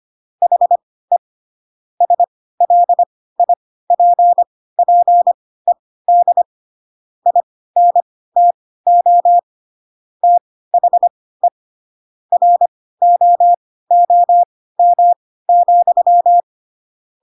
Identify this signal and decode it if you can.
Morse code, heard as a keyed tone, about 25 wpm, spaced as for 16 wpm, 700 Hz. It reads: HE SLIPPED INTO THE ROOM,
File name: Pinocchio / Pinocchio ch4 10 25 ik1hge